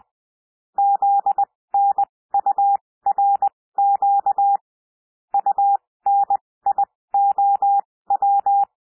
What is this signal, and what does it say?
Morse code, heard as a keyed tone, about 20 wpm, spaced as for 16 wpm, 800 Hz. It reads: ZNURQ UNIOW
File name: cwmp3 / L1 SOUND LETTERS 20 wpm